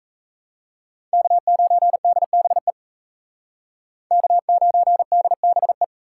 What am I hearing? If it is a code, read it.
Morse code, heard as a keyed tone, about 35 wpm, 700 Hz. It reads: K9DBE K9DBE
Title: Speed Racing - USA 1x3 Call Signs 28wpm